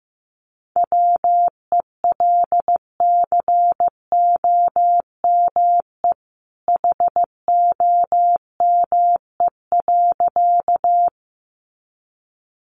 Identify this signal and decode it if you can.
Morse code, heard as a keyed tone, about 15 wpm, 700 Hz. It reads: WELCOME HOME.
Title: A Sound Of Thunder 15wpm